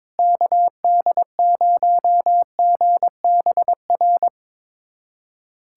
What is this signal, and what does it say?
Morse code, heard as a keyed tone, about 22 wpm, 700 Hz. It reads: KD0GBR
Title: USA Callsign DE Callsign 22wpm